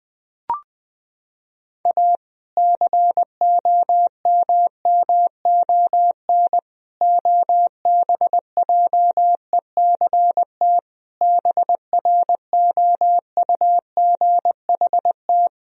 Morse code, heard as a keyed tone, about 20 wpm, 700 Hz. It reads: A COMMON OBJECT BROUGHT
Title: Sentences from Top 1000 Words - Review Part 3 - 20wpm